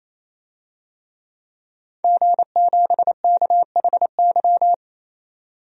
Morse code, heard as a keyed tone, about 28 wpm, 700 Hz. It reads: G7KHY
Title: DXCC-100 Call Signs - 5 Characters - 28wpm